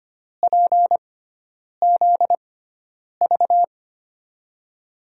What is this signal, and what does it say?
Morse code, heard as a keyed tone, about 25 wpm, spaced as for 9 wpm, 700 Hz. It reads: PZV
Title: USA Call Signs - 6 Characters - 6x Character Spacing - 25wpm